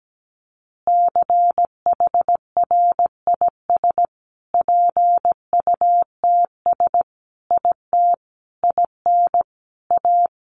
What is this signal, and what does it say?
Morse code, heard as a keyed tone, about 17 wpm, 700 Hz. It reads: CHRIS PUTS IT IN A
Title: Additional Easy Phrases 17wpm